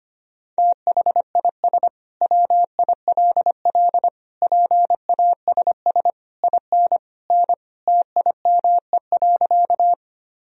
Morse code, heard as a keyed tone, about 25 wpm, 700 Hz. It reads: THIS WILL PASS IN N TIME.
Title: Sentences from Top 400 Words and Abbreviations 25wpm